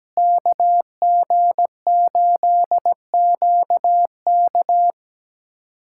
Morse code, heard as a keyed tone, about 17 wpm, 700 Hz. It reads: KG8QK
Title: USA Callsign DE Callsign - 5 Characters - 17wpm